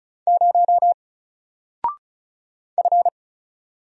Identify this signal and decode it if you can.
Morse code, heard as a keyed tone, about 36 wpm, 700 Hz. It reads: 0 F
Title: Single Character (Letter:Number:Punctuation:Prosign) Alt 35wpm